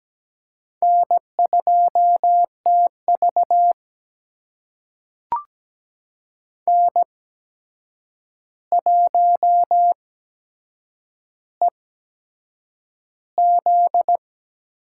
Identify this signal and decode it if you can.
Morse code, heard as a keyed tone, about 17 wpm, spaced as for 7 wpm, 700 Hz. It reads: N2TV N1EZ
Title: USA Call Signs - 8x Character Spacing - 17wpm